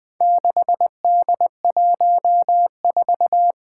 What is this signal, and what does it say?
Morse code, heard as a keyed tone, about 21 wpm, 700 Hz. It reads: 6D14
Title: Four Challenging Letter-Number Combinations 20wpm